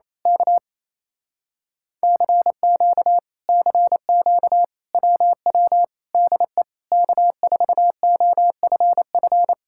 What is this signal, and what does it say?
Morse code, heard as a keyed tone, about 26 wpm, 700 Hz. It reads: K CQ CQ WW DE K4OFF
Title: Speed-Racing Common QSO Phrases - 28wpm